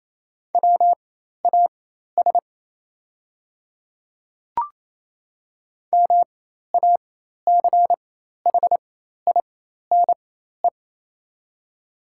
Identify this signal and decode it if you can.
Morse code, heard as a keyed tone, about 28 wpm, spaced as for 13 wpm, 700 Hz. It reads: WAS MACHINE
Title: 4x Character Spacing - Top 500 Words - 28wpm